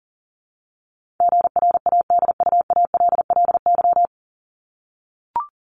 Morse code, heard as a keyed tone, about 40 wpm, 700 Hz. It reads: GRADUALLY
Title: LY Suffix - Top 100 Words with Base Words 40wpm